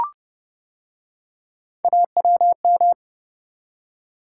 Morse code, heard as a keyed tone, about 29 wpm, 700 Hz. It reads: AWM